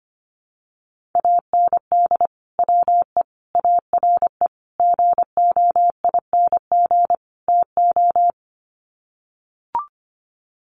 Morse code, heard as a keyed tone, about 25 wpm, 700 Hz. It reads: AND WE ARE GOING TO
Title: Sets of 5 Words Encore2 25wpm